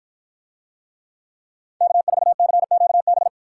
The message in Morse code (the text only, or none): K4CQB